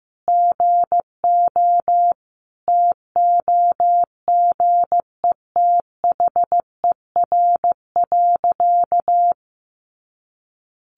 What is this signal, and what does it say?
Morse code, heard as a keyed tone, about 14 wpm, 700 Hz. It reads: GO TOGETHER.
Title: Sentences from Top 1500 Words 15wpm